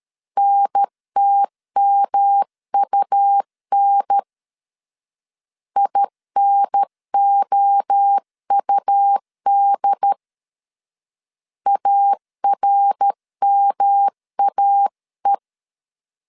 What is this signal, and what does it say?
Morse code, heard as a keyed tone, about 13 wpm, 800 Hz. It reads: NTMUN INOUD ARMAE
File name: CW / DET / 1 3 1500C S A 12wpm 2905